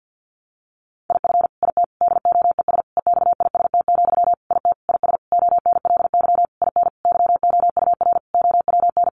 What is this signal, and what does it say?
Morse code, heard as a keyed tone, about 50 wpm, 700 Hz. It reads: IF IT DOES EXIST, IT IS ONLY IN YOUR OWN
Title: Sentences from Top 1300 Words 50wpm